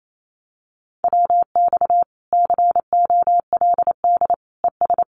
Morse code, heard as a keyed tone, about 28 wpm, 700 Hz. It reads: WX COLD ES
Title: Common QSO Phrases 28wpm